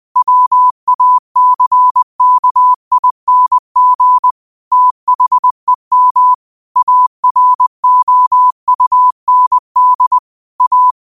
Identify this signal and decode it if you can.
Morse code, heard as a keyed tone, about 20 wpm, 1000 Hz. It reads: WACKING THEM AROUND A
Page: 20 wpm morse code quotes